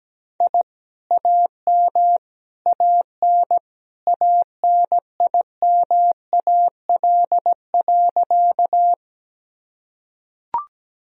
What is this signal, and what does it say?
Morse code, heard as a keyed tone, about 17 wpm, 700 Hz. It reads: I AM AN ANIMAL.
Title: Sentences from Top 1000 Words - Review Part 1 - 17wpm